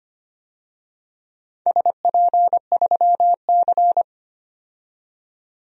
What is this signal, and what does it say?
Morse code, heard as a keyed tone, about 25 wpm, 700 Hz. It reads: SP3C
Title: DXCC-100 Call Signs - 4 Characters - 25wpm